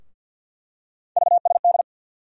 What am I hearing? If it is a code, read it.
Morse code, heard as a keyed tone, about 49 wpm, 690 Hz. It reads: VSD
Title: USA Call Sign Postfixes - 3 Letters 50wpm